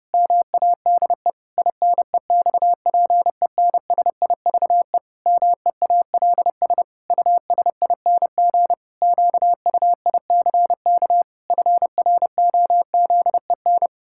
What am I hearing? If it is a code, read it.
Morse code, heard as a keyed tone, about 28 wpm, 700 Hz. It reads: MADE INEXPENSIVE MEALS USING QUICK FROZEN
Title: Pangram Challenge v2 - 30wpm